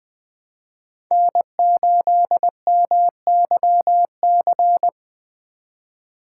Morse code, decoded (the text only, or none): N8MYC